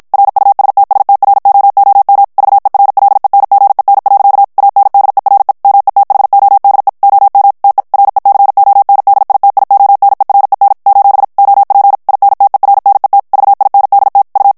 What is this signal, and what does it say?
Morse code, heard as a keyed tone, about 56 wpm, 800 Hz. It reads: RUSTSTROOM VERLENGEN. ANDERE METHODE OM TE RECONDITIONEREN : OP INTERNET VINDT U
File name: HOLLANDAIS texte clair 2 vitesse 50 mots minute